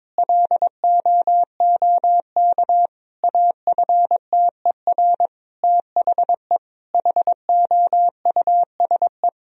Morse code, heard as a keyed tone, about 22 wpm, 700 Hz. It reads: LOOK AFTER THE HOUSE
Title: Sentences from Top 1000 Words - Review Part 4 - 22wpm